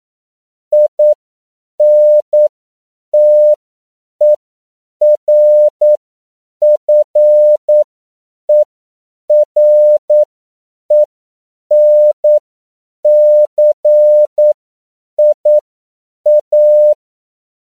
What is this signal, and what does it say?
Morse code, heard as a keyed tone, about 9 wpm, spaced as for 7 wpm, 600 Hz. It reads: INTERFERENCIA
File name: CW / 10PPM